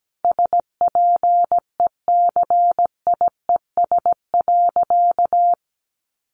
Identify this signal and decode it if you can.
Morse code, heard as a keyed tone, about 17 wpm, 700 Hz. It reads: SPECIES.
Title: A Sound Of Thunder 17wpm